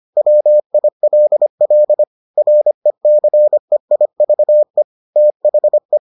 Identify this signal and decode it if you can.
Morse code, heard as a keyed tone, about 25 wpm, 600 Hz. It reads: WILL RECEIVE THE